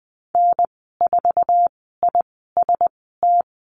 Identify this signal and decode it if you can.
Morse code, heard as a keyed tone, about 20 wpm, spaced as for 15 wpm, 700 Hz. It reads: N4IST
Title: 088 Lesson 15g - Number 5 - Call Signs 20wpm